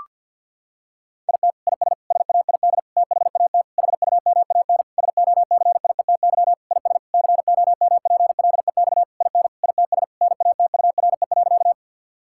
Morse code, heard as a keyed tone, about 49 wpm, 700 Hz. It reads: IT IS SAID THAT HUMAN SOCIETY IS COMPLEX IN ITS NATURE.
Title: Sentences from Top 1000 Words - Review Part 6 - 50wpm